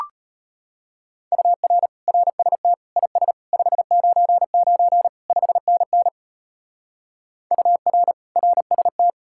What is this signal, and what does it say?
Morse code, heard as a keyed tone, about 31 wpm, 700 Hz. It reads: UR RST IS 599 5NN UR RST
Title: Speed-Racing Common QSO Phrases - 25wpm